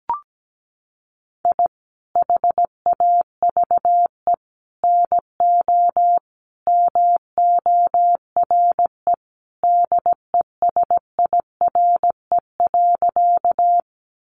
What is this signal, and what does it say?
Morse code, heard as a keyed tone, about 17 wpm, 700 Hz. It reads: I HAVE NO MORE DESIRE.
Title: Sentences from Top 1100 Words 17wpm